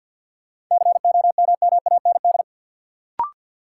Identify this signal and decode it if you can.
Morse code, heard as a keyed tone, about 46 wpm, 700 Hz. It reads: COMMAND
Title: Sentences from Top 1600 Words - New Word Drill Alt - 50wpm